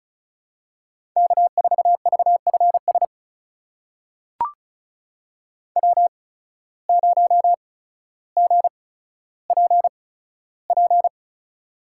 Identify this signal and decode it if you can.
Morse code, heard as a keyed tone, about 36 wpm, spaced as for 15 wpm, 700 Hz. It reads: K4VFS W0GPP
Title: USA Call Signs - 5 Characters - 8x Character Spacing - 35wpm